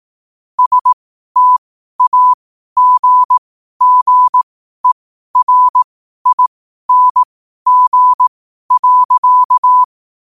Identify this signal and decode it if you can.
Morse code, heard as a keyed tone, about 18 wpm, spaced as for 12 wpm, 1000 Hz. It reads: STAGGERING.
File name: qotdfiles / 12wpm